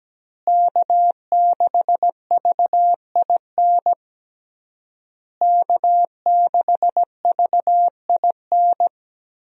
Morse code, heard as a keyed tone, about 17 wpm, 700 Hz. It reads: K6VIN K6VIN
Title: USA Call Signs - 5 Characters - Repeated 3x - 17wpm